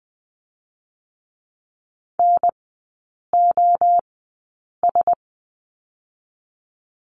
Morse code, heard as a keyed tone, about 20 wpm, 700 Hz. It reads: N O S
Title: 047 Lesson 9e - Number 4 - Three Characters 20wpm